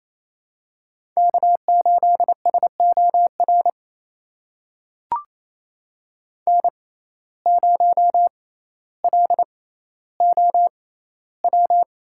Morse code, heard as a keyed tone, about 28 wpm, spaced as for 14 wpm, 700 Hz. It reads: K8SOR N0LOW
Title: USA Call Signs - 5 Characters - 6x Character Spacing - 28wpm